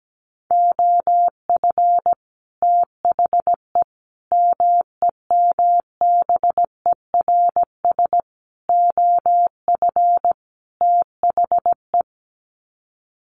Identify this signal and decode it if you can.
Morse code, heard as a keyed tone, about 17 wpm, 700 Hz. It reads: OF THE MEMBERS OF THE
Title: Sets of 5 Words Encore2 17wpm